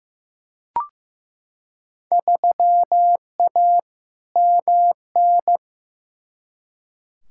Morse code, heard as a keyed tone, about 15 wpm, 700 Hz. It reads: 3A MN
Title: Field Day Exchanges - 15wpm